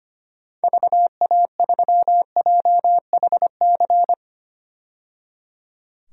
Morse code, heard as a keyed tone, about 25 wpm, 700 Hz. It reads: VA3JHC
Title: DXCC-100 Call Signs - 6 Characters - 25wpm